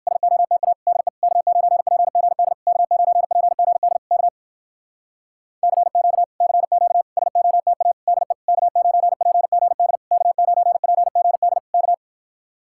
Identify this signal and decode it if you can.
Morse code, heard as a keyed tone, about 59 wpm, 700 Hz. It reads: SOTA DE K9PGD K9PGD K CQ CQ SOTA DE K9PGD K9PGD K